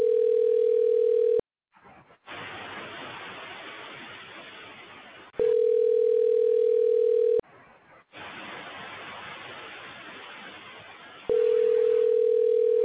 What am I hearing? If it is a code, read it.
Ringback tone (North America), 440 + 480 Hz.